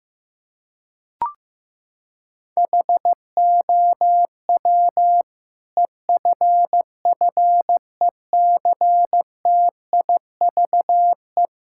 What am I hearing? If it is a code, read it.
Morse code, heard as a keyed tone, about 15 wpm, 700 Hz. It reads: HOW EFFECTIVE
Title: Sentences from Top 1000 Words - Review Part 6 - 15wpm